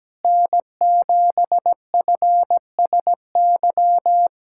Morse code, heard as a keyed tone, about 17 wpm, 700 Hz. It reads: N7FSY
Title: USA Callsign DE Callsign - 5 Characters - 17wpm